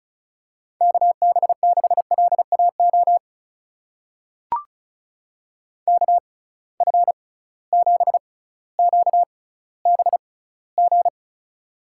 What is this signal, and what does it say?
Morse code, heard as a keyed tone, about 36 wpm, spaced as for 18 wpm, 700 Hz. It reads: KB6LAO KF7QBG
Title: USA Call Signs - 6 Characters - 6x Character Spacing - 35wpm